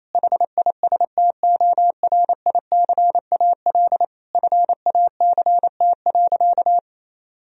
Morse code, heard as a keyed tone, about 27 wpm, 700 Hz. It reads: HISTORICAL FACT.
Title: Sentences from Top 1000 Words - Review Part 4 - 28wpm